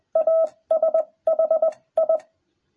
Morse code, heard as a keyed tone, about 21 wpm, spaced as for 18 wpm, 660 Hz. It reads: ASH I